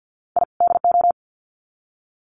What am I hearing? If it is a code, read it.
Morse code, heard as a keyed tone, about 50 wpm, 700 Hz. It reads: I DO
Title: Sets of 2 Words 50wpm